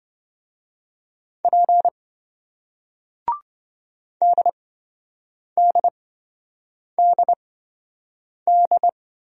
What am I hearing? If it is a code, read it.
Morse code, heard as a keyed tone, about 27 wpm, spaced as for 7 wpm, 700 Hz. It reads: P DDDD